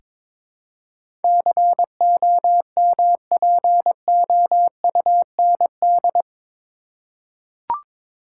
Morse code, heard as a keyed tone, about 22 wpm, 700 Hz. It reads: COMPOUND
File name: practice / 22wpm / Eight Letter Words - 22wpm